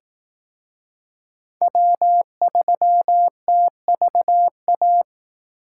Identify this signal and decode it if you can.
Morse code, heard as a keyed tone, about 19 wpm, 700 Hz. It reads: W3TVA